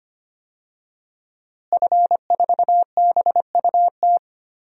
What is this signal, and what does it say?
Morse code, heard as a keyed tone, about 25 wpm, 700 Hz. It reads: F4BUT
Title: DXCC-100 Call Signs - 5 Characters - Repeated 3x - 25wpm